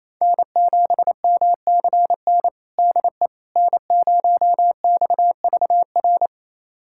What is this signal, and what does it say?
Morse code, heard as a keyed tone, about 28 wpm, 690 Hz. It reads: N7MCN DE N0XVR
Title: USA Callsign DE Callsign - 5 Characters - 28wpm